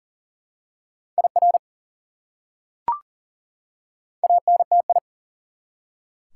Morse code, heard as a keyed tone, about 40 wpm, 700 Hz. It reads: IR ANTI
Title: Prefixes and Suffixes 40wpm